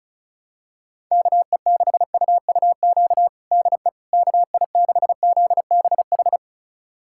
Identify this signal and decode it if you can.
Morse code, heard as a keyed tone, about 36 wpm, 700 Hz. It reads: KE6UUQ DE KI6ZBH
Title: USA Callsign DE Callsign 35wpm